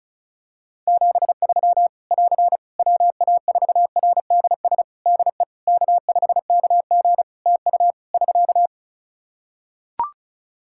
Morse code, heard as a keyed tone, about 34 wpm, 700 Hz. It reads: 73 <AR> WA4RDS DE K5KG TU <SK>
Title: Practice QSO - Exchange 4 - Ending - 35wpm